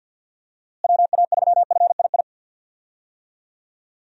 Morse code, heard as a keyed tone, about 48 wpm, 700 Hz. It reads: WA3FII
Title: USA Call Signs - 6 Characters - Repeated 2x - 50wpm